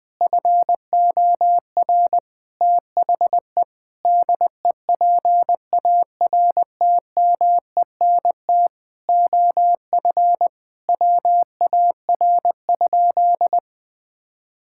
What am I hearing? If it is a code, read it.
Morse code, heard as a keyed tone, about 21 wpm, 700 Hz. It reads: FOR THE DEPARTMENT OF WAR?
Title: Sentences from Top 1000 Words - Review Part 5 - 20wpm